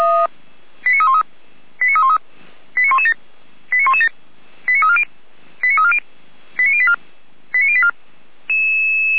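Five-tone selective call (ZVEI), heard as quick runs of five tones, tones of about 70 ms. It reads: 89313 89313 89117 89117 89340 89340 89084 89084
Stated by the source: ZVEI five-tone selective call 89167 received by SDR, fire brigade Feuerwehr Orlamünde.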